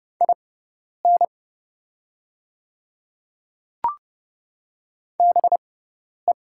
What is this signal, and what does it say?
Morse code, heard as a keyed tone, about 30 wpm, spaced as for 10 wpm, 700 Hz. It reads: IN BE
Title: 6x Character Spacing - Top 500 Words - 30wpm